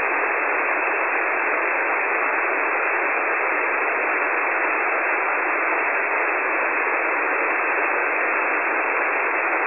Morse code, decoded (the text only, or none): none